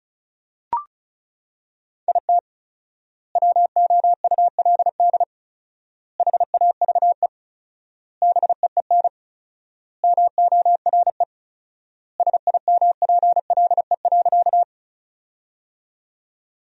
Morse code, heard as a keyed tone, about 36 wpm, 700 Hz. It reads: IT WOULD HAVE BEEN MORE SIMPLE.
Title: 4x Word-Spacing - Sentences from Top 400 Words - 35wpm